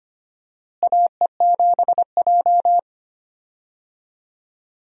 Morse code, heard as a keyed tone, about 25 wpm, 700 Hz. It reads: AE7J
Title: USA Call Signs - 4 Characters - Repeated 2x - 25wpm